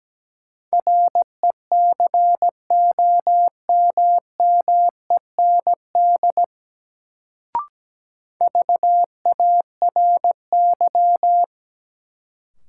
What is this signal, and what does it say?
Morse code, heard as a keyed tone, about 17 wpm, 700 Hz. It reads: RECOMMEND VARY
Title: Sentences from Top 1200 Words - New Word Drill - 17wpm